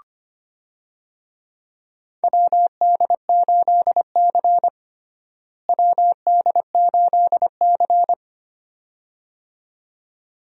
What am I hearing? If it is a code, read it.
Morse code, heard as a keyed tone, about 25 wpm, 700 Hz. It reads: WD8C WD8C
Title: USA Call Signs - 4 Characters - Repeated 2x - 25wpm